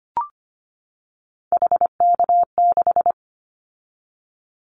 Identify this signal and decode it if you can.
Morse code, heard as a keyed tone, about 25 wpm, 700 Hz. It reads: HK6
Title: Callsign Prefixes for Top 100 DXCC - 25wpm